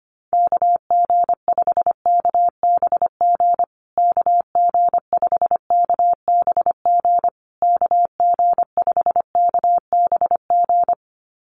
Morse code, heard as a keyed tone, about 25 wpm, 700 Hz. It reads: KG5KBG KG5KBG KG5KBG
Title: USA Call Signs Repeated 3x 25wpm